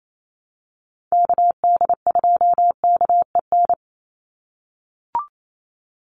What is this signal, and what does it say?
Morse code, heard as a keyed tone, about 28 wpm, 700 Hz. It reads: KD2KEN
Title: USA Call Signs 28wpm